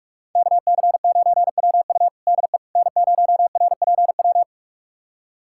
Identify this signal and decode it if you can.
Morse code, heard as a keyed tone, about 41 wpm, 700 Hz. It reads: KC9WU DE N0RPW
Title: USA Callsign DE Callsign - 5 Characters - 45wpm